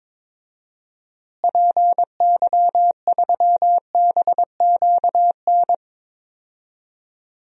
Morse code, heard as a keyed tone, about 22 wpm, 700 Hz. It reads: PY3BQN
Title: DXCC-100 Call Signs - 6 Characters - Repeated 2x - 22wpm